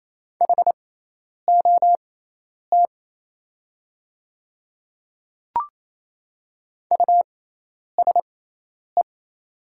Morse code, heard as a keyed tone, about 27 wpm, spaced as for 10 wpm, 700 Hz. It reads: HOT USE
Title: 6x Character Spacing - Top 100 Words - 28wpm